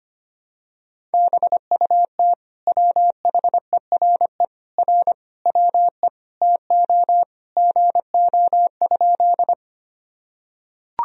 Morse code, heard as a keyed tone, about 25 wpm, 700 Hz. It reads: BUT WHERE R WE TO GO?
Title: Sentences from Top 200 Words and Abbreviations 25wpm